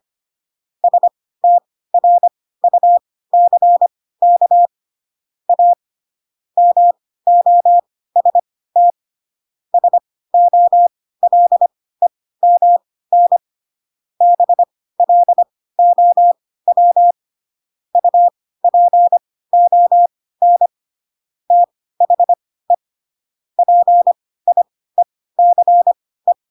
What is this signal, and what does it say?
Morse code, heard as a keyed tone, about 25 wpm, spaced as for 16 wpm, 700 Hz. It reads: STRUCK A MOST SOLEMN BLOW UPON THE PIECE
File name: Pinocchio / Pinocchio ch1 10 25 ik1hge